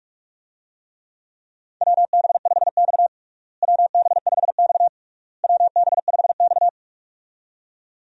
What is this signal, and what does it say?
Morse code, heard as a keyed tone, about 41 wpm, 700 Hz. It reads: WB5X WB5X WB5X
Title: USA Call Signs - 4 Characters - Repeated 3x - 45wpm